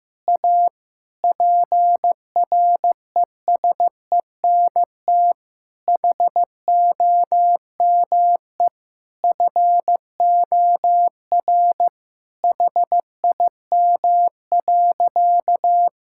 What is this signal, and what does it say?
Morse code, heard as a keyed tone, about 15 wpm, 700 Hz. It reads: A PRESENT HOME FOR HIM.
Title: Sentences from Top 1000 Words - Review Part 5 - 15wpm